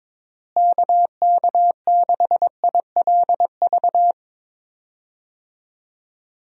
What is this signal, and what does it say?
Morse code, heard as a keyed tone, about 22 wpm, 700 Hz. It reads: KK6ILV
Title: USA Call Signs - 6 Characters - Repeated 2x - 22wpm